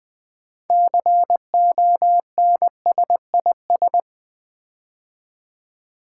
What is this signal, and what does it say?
Morse code, heard as a keyed tone, about 20 wpm, 700 Hz. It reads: CONSIS
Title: Build A Word - 6 Letters At A Time - 12-Letter Words - 20wpm